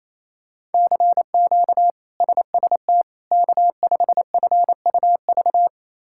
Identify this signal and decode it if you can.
Morse code, heard as a keyed tone, about 28 wpm, 700 Hz. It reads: CQ SST K5FUV
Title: K1USN SST QSO Practice - 28wpm